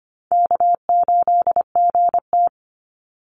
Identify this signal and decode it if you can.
Morse code, heard as a keyed tone, about 25 wpm, 700 Hz. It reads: K8GT
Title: Call Signs - 25wpm